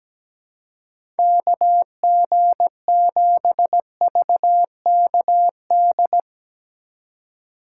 Morse code, heard as a keyed tone, about 17 wpm, 700 Hz. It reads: KG7VKD
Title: USA Call Signs - 6 Characters - 2x Character Spacing - 17wpm